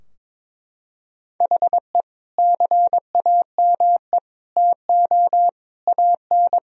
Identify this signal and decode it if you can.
Morse code, heard as a keyed tone, about 22 wpm, 700 Hz. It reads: HE CAME TO AN
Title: Sentences from Top 1000 Words - Review Part 3 - 22wpm